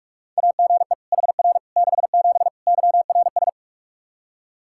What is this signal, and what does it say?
Morse code, heard as a keyed tone, about 42 wpm, 700 Hz. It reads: AGE HR 67 YRS
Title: K1USN SST QSO Practice 45wpm